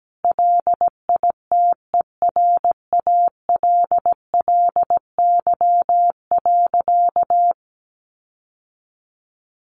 Morse code, heard as a keyed tone, about 17 wpm, 700 Hz. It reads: LITERALLY.
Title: Sentences from Top 1400 Words 17wpm